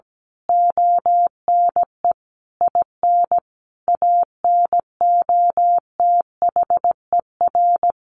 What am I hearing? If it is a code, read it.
Morse code, heard as a keyed tone, about 17 wpm, 700 Hz. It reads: ONE IN ANOTHER
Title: A Sound Of Thunder 17wpm